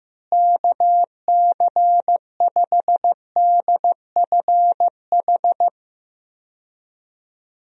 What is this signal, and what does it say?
Morse code, heard as a keyed tone, about 15 wpm, 700 Hz. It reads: KC5DFH
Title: USA Call Signs - 6 Characters - Repeated 3x - 15wpm